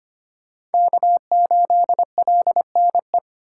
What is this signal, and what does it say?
Morse code, heard as a keyed tone, about 25 wpm, 700 Hz. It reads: K8LNE
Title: USA Call Signs - 5 Characters - 4x Character Spacing - 25wpm